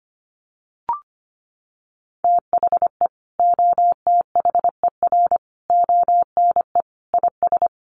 Morse code, heard as a keyed tone, about 25 wpm, 700 Hz. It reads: THE OTHER ONE IS
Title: Additional Easy Phrases 25wpm